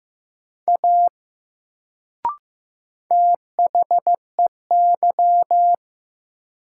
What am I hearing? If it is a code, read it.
Morse code, heard as a keyed tone, about 15 wpm, 700 Hz. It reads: A THEY
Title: Sentences from Top 100 Words - New Word Drill - 15wpm